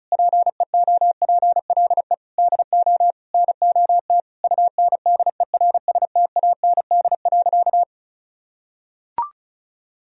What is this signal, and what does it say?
Morse code, heard as a keyed tone, about 36 wpm, 700 Hz. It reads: PEOPLE DO NOT UNDERSTAND.
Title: Sentences from Top 1000 Words - Review Part 1 - 35wpm